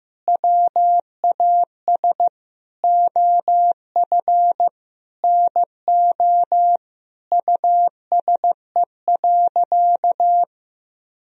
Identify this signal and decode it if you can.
Morse code, heard as a keyed tone, about 15 wpm, 700 Hz. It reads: WAS OF NO USE.
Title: Sentences from Top 1000 Words - Review Part 4 - 15wpm